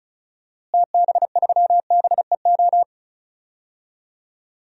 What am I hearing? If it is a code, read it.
Morse code, heard as a keyed tone, about 34 wpm, 700 Hz. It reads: TB3BEO
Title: DXCC-100 Call Signs - 6 Characters - Repeated 2x - 35wpm